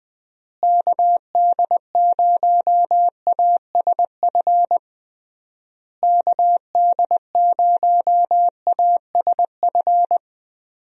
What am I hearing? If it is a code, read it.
Morse code, heard as a keyed tone, about 20 wpm, 700 Hz. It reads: KD0ASF KD0ASF
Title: USA Call Signs - 6 Characters - Repeated 3x - 20wpm